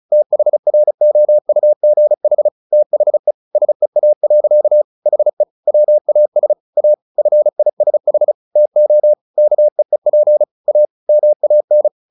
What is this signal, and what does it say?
Morse code, heard as a keyed tone, about 34 wpm, 600 Hz. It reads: THROUGH THE SEA. HE WAS A FISH TO KEEP A MAN
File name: the old man and the sea 2 35wpm